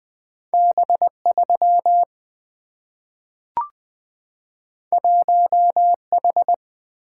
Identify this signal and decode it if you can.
Morse code, heard as a keyed tone, about 20 wpm, 690 Hz. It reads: B3 1H
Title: Two Letter-Number Combinations - 20wpm